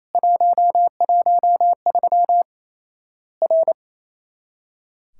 Morse code, heard as a keyed tone, about 28 wpm, 700 Hz. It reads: 113 R
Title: Worked All Europe DX Contest - QTC Exchange - 28wpm